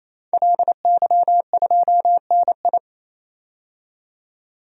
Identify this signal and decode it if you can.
Morse code, heard as a keyed tone, about 28 wpm, 700 Hz. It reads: LY2NI
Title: DXCC-100 Call Signs - 5 Characters - 28wpm